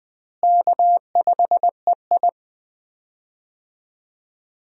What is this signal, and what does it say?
Morse code, heard as a keyed tone, about 20 wpm, 700 Hz. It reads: K5EI
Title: USA Call Signs - 4 Characters - Repeated 3x - 20wpm